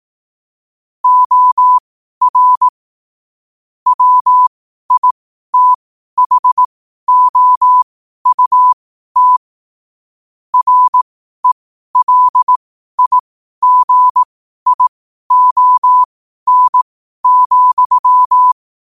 Morse code, heard as a keyed tone, about 18 wpm, spaced as for 12 wpm, 1000 Hz. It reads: OR WITHOUT RELIGION,
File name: qotdfiles / 12wpm